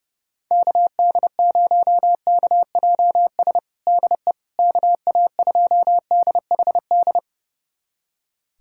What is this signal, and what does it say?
Morse code, heard as a keyed tone, about 30 wpm, 700 Hz. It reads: KD0KJS DE KA2DHD
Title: USA Callsign DE Callsign - 6 Characters - 30wpm